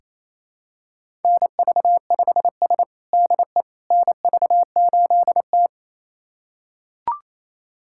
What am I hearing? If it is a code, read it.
Morse code, heard as a keyed tone, about 27 wpm, 700 Hz. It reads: NV5S DE NV8T
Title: USA Callsign DE Callsign - 4 Characters - 28wpm